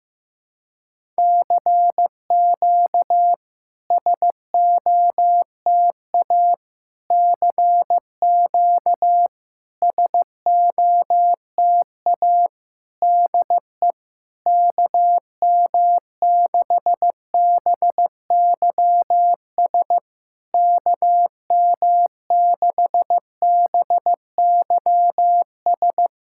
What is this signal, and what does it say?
Morse code, heard as a keyed tone, about 15 wpm, 690 Hz. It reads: CQ SOTA CQ SOTA DE KM6BYS KM6BYS
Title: SOTA QSO Practice 15wpm